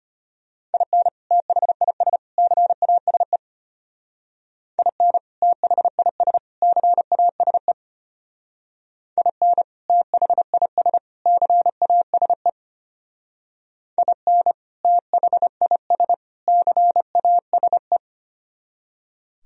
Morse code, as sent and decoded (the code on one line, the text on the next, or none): .. -. / - .... .. ... / -.-. .- ... / .. -. / - .... .. ... / -.-. .- ... / .. -. / - .... .. ... / -.-. .- ... / .. -. / - .... .. ... / -.-. .- ... .
IN THIS CASE IN THIS CASE IN THIS CASE IN THIS CASE